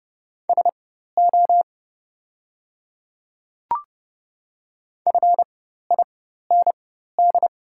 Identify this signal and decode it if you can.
Morse code, heard as a keyed tone, about 30 wpm, spaced as for 14 wpm, 700 Hz. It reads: SO FIND
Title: 4x Character Spacing - Top 100 Words - 30wpm